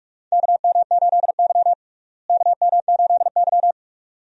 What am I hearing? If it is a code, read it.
Morse code, heard as a keyed tone, about 46 wpm, 700 Hz. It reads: KM8Y KM8Y
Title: USA Call Signs - 4 Characters - Repeated 2x - 45wpm